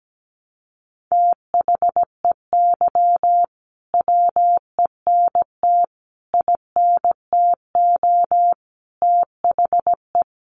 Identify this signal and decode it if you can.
Morse code, heard as a keyed tone, about 17 wpm, 700 Hz. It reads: THEY WENT INTO THE
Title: Sentences from Top 900 Words 17wpm